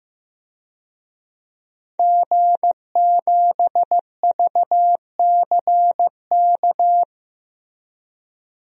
Morse code, heard as a keyed tone, about 15 wpm, 700 Hz. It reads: G7VCK
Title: DXCC-100 Call Signs - 5 Characters - 15wpm